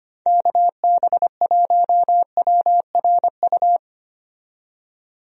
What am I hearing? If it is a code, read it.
Morse code, heard as a keyed tone, about 25 wpm, 700 Hz. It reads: KB1WRU
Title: USA Call Signs - 6 Characters - 25wpm